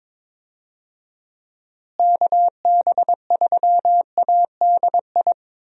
Morse code, heard as a keyed tone, about 22 wpm, 700 Hz. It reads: KB3ADI
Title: USA Call Signs - 6 Characters - Repeated 2x - 22wpm